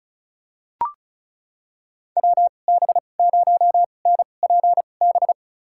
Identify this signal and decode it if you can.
Morse code, heard as a keyed tone, about 36 wpm, spaced as for 25 wpm, 700 Hz. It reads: WB0NPB
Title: USA Call Signs - 6 Characters - 2x Character Spacing - 35wpm